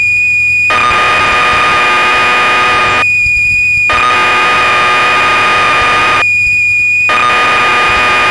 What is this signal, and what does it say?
Cassette data (Kansas City type).